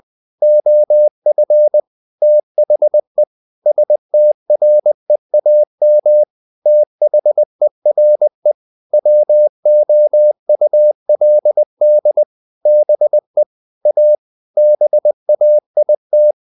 Morse code, heard as a keyed tone, about 20 wpm, 600 Hz. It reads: OF THE STREAM THERE WOULD BE A BAIT
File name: the old man and the sea / the old man and the sea part 1 20wpm